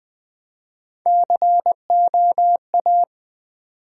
Morse code, heard as a keyed tone, about 20 wpm, 700 Hz. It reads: COA
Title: Build A Word - 1 Letter At A Time - 4-Letter Words - 20wpm